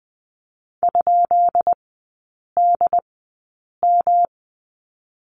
Morse code, heard as a keyed tone, about 20 wpm, spaced as for 8 wpm, 700 Hz. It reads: ?DM
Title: 133 Lesson 22e - Question Mark - Three Characters 20wpm